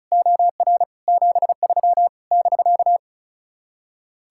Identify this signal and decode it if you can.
Morse code, heard as a keyed tone, about 43 wpm, 700 Hz. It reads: OR 73 <BK>
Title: Speed-Racing Common QSO Phrases Encore - 35wpm